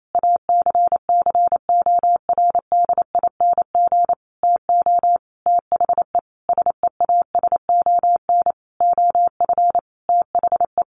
Morse code, heard as a keyed tone, about 28 wpm, 700 Hz. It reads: ACCORDING TO THE SEASON OF THE
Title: Sentences from Top 1400 Words 28wpm